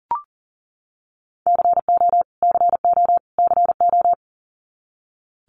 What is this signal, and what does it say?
Morse code, heard as a keyed tone, about 40 wpm, 700 Hz. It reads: CO CO CO
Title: US States Repeated 3x 40wpm